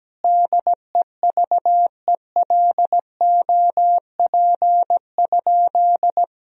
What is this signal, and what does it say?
Morse code, heard as a keyed tone, about 17 wpm, 700 Hz. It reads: DEVELOP?
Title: Sentences from Top 1000 Words - Review Part 2 - 17wpm